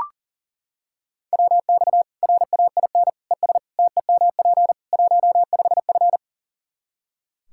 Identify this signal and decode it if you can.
Morse code, heard as a keyed tone, about 40 wpm, 700 Hz. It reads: WX RAIN ES TEMP 15F
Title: Practice QSO - Exchange 2 - Info - 40wpm